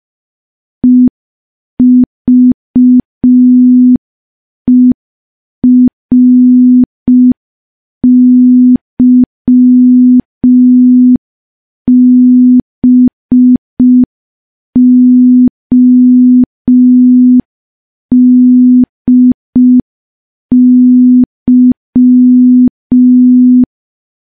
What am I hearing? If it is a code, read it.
Morse code, heard as a keyed tone, about 5 wpm, 250 Hz. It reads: EVERYBODY
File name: morse / morse-training-05WPM